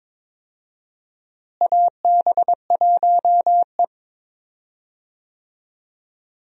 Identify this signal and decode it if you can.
Morse code, heard as a keyed tone, about 22 wpm, 700 Hz. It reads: AB1E
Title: USA Call Signs - 4 Characters - Repeated 3x - 22wpm